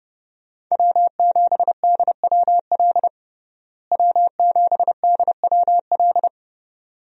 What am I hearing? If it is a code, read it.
Morse code, heard as a keyed tone, about 30 wpm, 700 Hz. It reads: W7DWL W7DWL
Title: USA Call Signs - 5 Characters - Repeated 2x - 30wpm